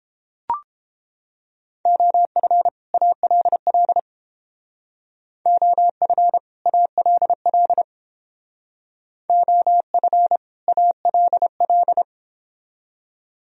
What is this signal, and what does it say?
Morse code, heard as a keyed tone, about 28 wpm, 700 Hz. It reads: OF ALL OF ALL OF ALL